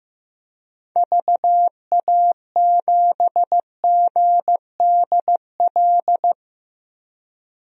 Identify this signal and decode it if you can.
Morse code, heard as a keyed tone, about 15 wpm, 700 Hz. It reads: VA7GDL
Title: DXCC-100 Call Signs - 6 Characters - 15wpm